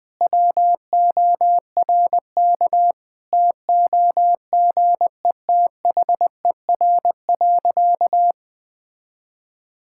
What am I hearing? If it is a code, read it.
Morse code, heard as a keyed tone, about 20 wpm, 700 Hz. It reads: WORK TOGETHER.
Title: Sentences with 7-Word Phrases - 20wpm